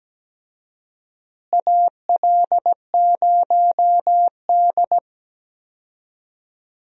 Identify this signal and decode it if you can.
Morse code, heard as a keyed tone, about 17 wpm, 700 Hz. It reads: AL0D
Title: USA Call Signs - 4 Characters - Repeated 2x - 17wpm